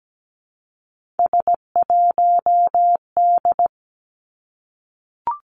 Morse code, heard as a keyed tone, about 17 wpm, 700 Hz. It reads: S1D
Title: Three Challenging Letter-Number Combinations 17wpm